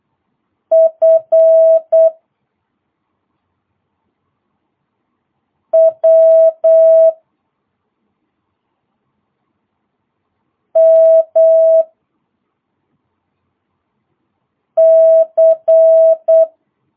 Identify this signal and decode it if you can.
Morse code, heard as a keyed tone, about 8 wpm, 650 Hz. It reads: F W M C